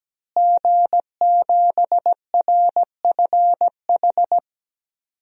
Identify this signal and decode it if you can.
Morse code, heard as a keyed tone, about 17 wpm, 700 Hz. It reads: G7RFH
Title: DXCC-100 Call Signs - 5 Characters - Repeated 2x - 17wpm